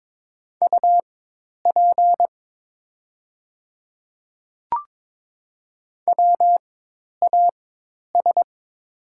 Morse code, heard as a keyed tone, about 22 wpm, spaced as for 10 wpm, 700 Hz. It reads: UP WAS
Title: 4x Character Spacing - Top 100 Words - 22wpm